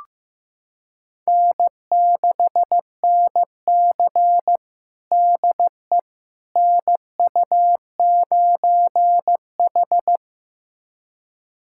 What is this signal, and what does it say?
Morse code, heard as a keyed tone, about 16 wpm, 700 Hz. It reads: N6NC DE NU9H
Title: USA Callsign DE Callsign - 4 Characters - 15wpm